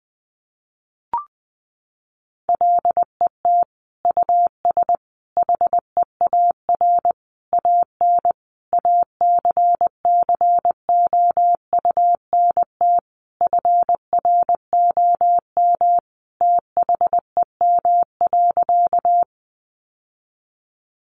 Morse code, heard as a keyed tone, about 19 wpm, 700 Hz. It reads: LET US HEAR AN ACCOUNT FROM THEM.
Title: Sentences from Top 900 Words 20wpm